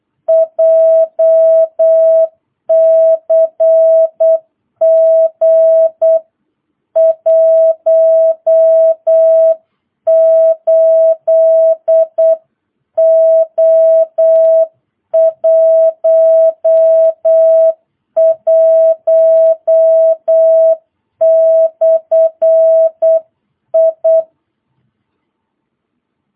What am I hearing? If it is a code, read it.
Morse code, heard as a keyed tone, about 8 wpm, 660 Hz. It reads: JCG 18O11/I